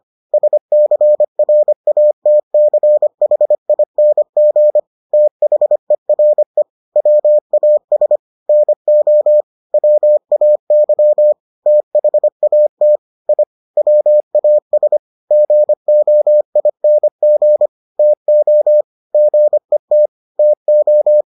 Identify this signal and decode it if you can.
Morse code, heard as a keyed tone, about 25 wpm, 600 Hz. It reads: SCRATCHING THERE WAS NO WAY THAT I WAS GOING TO GET TO